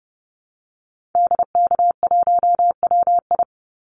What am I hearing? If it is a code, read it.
Morse code, heard as a keyed tone, about 30 wpm, 700 Hz. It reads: DK1WI
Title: Call Signs - 30wpm